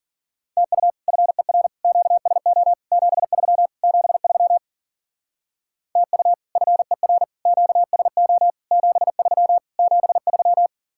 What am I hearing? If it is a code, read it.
Morse code, heard as a keyed tone, about 49 wpm, 700 Hz. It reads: TU FER QSO 73 73 TU FER QSO 73 73
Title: Speed-Racing Common QSO Phrases - 40wpm